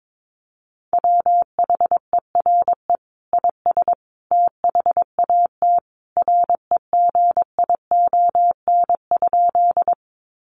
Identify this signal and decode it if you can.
Morse code, heard as a keyed tone, about 22 wpm, 700 Hz. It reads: WHERE IS THAT REGION?
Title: Sentences from Top 1500 Words 22wpm